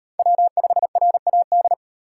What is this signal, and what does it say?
Morse code, heard as a keyed tone, about 38 wpm, 700 Hz. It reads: W5RAD